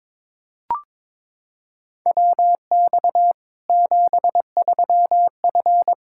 Morse code, heard as a keyed tone, about 21 wpm, 700 Hz. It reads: WX 73F
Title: Practice QSO - Exchange 2 - Info - 22wpm